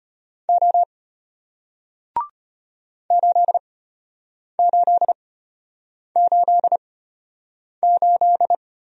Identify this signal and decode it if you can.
Morse code, heard as a keyed tone, about 34 wpm, spaced as for 8 wpm, 700 Hz. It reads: O 8888